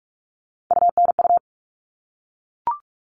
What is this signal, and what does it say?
Morse code, heard as a keyed tone, about 44 wpm, 700 Hz. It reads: UNU
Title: All Cut Numbers - 3-Digit 45wpm